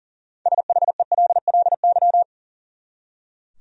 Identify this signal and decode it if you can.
Morse code, heard as a keyed tone, about 40 wpm, 700 Hz. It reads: SHELLY
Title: USA Common Names - 40wpm